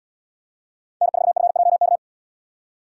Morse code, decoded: N4FJK

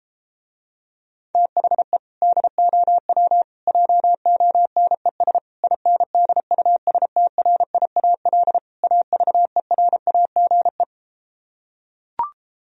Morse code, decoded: THE DOW JONES INDUSTRIAL AVERAGE

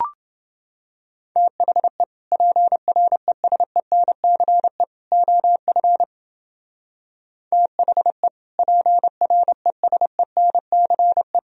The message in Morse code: THE PRESENCE OF THE PRESENCE

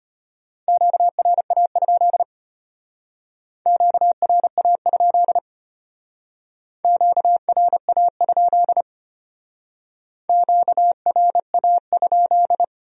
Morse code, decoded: QRA? QRA? QRA? QRA?

